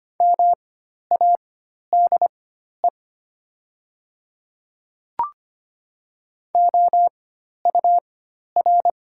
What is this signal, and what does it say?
Morse code, heard as a keyed tone, about 25 wpm, spaced as for 12 wpm, 700 Hz. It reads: MADE OUR
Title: 4x Character Spacing - Top 500 Words - 25wpm